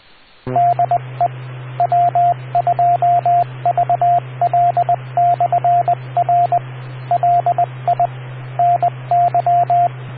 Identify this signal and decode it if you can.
Morse code, heard as a keyed tone, about 20 wpm, 700 Hz, with radio noise. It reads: DE W2VL/R LI NY